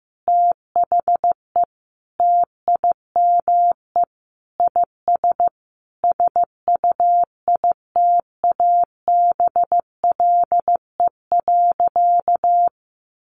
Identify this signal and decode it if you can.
Morse code, heard as a keyed tone, about 15 wpm, 700 Hz. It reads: THE TIME IS SUITABLE.
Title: Sentences from Top 1400 Words 15wpm